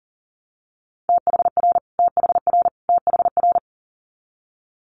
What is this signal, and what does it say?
Morse code, heard as a keyed tone, about 39 wpm, 700 Hz. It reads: THR THR THR
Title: Multi-letter Phonemes Repeated 3x 40wpm